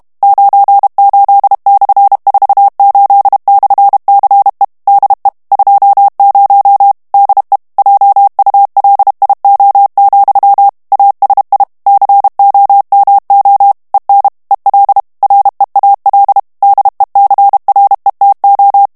Morse code, heard as a keyed tone, about 32 wpm, 800 Hz. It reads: 98/48/CE DE 20 DE JULIO, ASI COMO EN EL REAL DECRETO
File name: ESPAGNOL TEXTE EN CLAIR 2 VITESSE 30 MOTS MINUTES